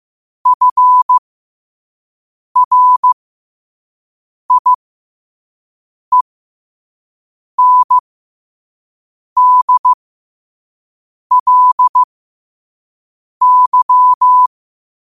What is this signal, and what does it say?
Morse code, heard as a keyed tone, about 15 wpm, spaced as for 5 wpm, 1000 Hz. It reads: FRIENDLY
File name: 5wpm